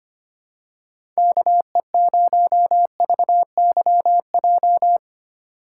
Morse code, heard as a keyed tone, about 25 wpm, 700 Hz. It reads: KE0VYJ
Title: USA Call Signs - 6 Characters - 6x Character Spacing - 25wpm